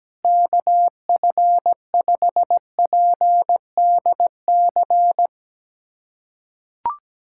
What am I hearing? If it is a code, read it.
Morse code, heard as a keyed tone, about 17 wpm, 700 Hz. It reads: KF5PDC
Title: USA Callsign DE Callsign - 6 Characters - 17wpm